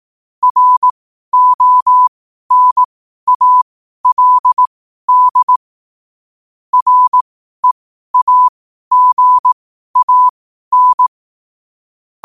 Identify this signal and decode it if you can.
Morse code, heard as a keyed tone, about 18 wpm, spaced as for 12 wpm, 1000 Hz. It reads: RONALD REAGAN
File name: qotdfiles / 12wpm